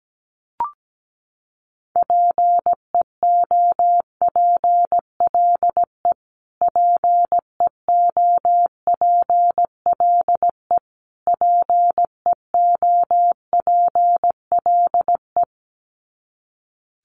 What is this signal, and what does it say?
Morse code, heard as a keyed tone, about 17 wpm, 700 Hz. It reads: PEOPLE PEOPLE PEOPLE
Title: Top 500 Words Repeated 17wpm